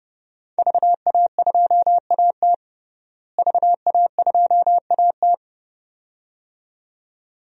Morse code, heard as a keyed tone, about 30 wpm, 700 Hz. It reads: VA2AT VA2AT
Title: DXCC-100 Call Signs - 5 Characters - Repeated 2x - 30wpm